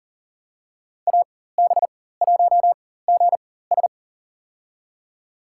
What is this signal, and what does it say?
Morse code, heard as a keyed tone, about 40 wpm, spaced as for 19 wpm, 700 Hz. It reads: AB1GS
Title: USA Call Signs - 5 Characters - 4x Character Spacing - 40wpm